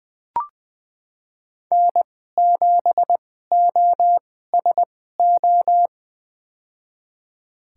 Morse code, heard as a keyed tone, about 20 wpm, spaced as for 15 wpm, 700 Hz. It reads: N7OSO
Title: USA Call Signs - 5 Characters - 2x Character Spacing - 20wpm